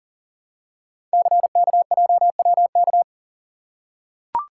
Morse code, heard as a keyed tone, about 40 wpm, 700 Hz. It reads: CKJWK
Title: Code Groups - 5 Letters - 40wpm